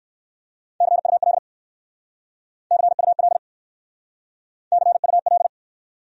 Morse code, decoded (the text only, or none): CUL CUL CUL